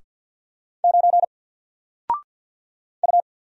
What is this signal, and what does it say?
Morse code, heard as a keyed tone, about 48 wpm, 700 Hz. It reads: 9 U
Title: Single Letter-Number 50wpm